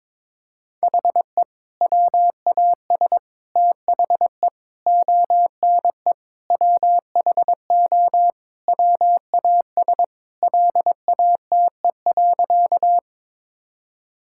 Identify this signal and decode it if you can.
Morse code, heard as a keyed tone, about 22 wpm, 700 Hz. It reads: HE WAS THE ONE WHO WAS LATE.
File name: practice / 22wpm / Sentences with 2-Word Phrases - 22wpm